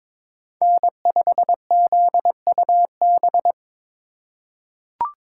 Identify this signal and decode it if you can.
Morse code, heard as a keyed tone, about 21 wpm, 700 Hz. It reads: N5ZUB
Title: USA Call Signs - 5 Characters - 4x Character Spacing - 22wpm